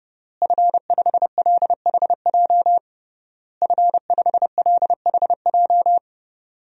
Morse code, heard as a keyed tone, about 31 wpm, 700 Hz. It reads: F5LHJ F5LHJ
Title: DXCC-100 Call Signs - 5 Characters - Repeated 3x - 30wpm